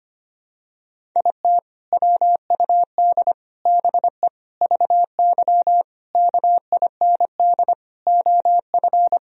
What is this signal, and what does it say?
Morse code, heard as a keyed tone, about 25 wpm, 700 Hz. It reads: IT WUD BE VY KIND OF
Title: Sentences from Top 200 Words and Abbreviations 25wpm